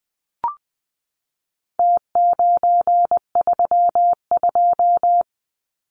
Morse code, heard as a keyed tone, about 20 wpm, 700 Hz. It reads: T932
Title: Numbers - Contest Style - 4-Digit 20wpm